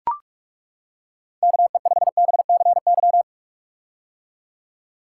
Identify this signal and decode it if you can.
Morse code, heard as a keyed tone, about 42 wpm, 700 Hz. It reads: KE5BCY
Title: K1USN SST QSO Practice - 45wpm